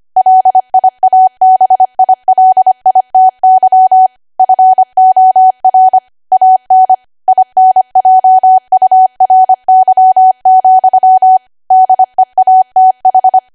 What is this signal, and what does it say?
Morse code, heard as a keyed tone, about 25 wpm, 750 Hz. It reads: LIABILITY FOR AN INJURY, DEATH
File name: W1AW-Level3/140218 25WPM